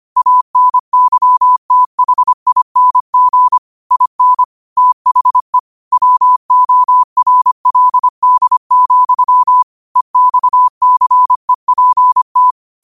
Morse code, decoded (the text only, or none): ANYTHING IN THE WORLD, EXCEPT